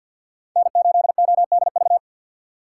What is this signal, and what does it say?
Morse code, heard as a keyed tone, about 47 wpm, 700 Hz. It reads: N8ODV
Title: USA Call Signs - 4x Character Spacing - 50wpm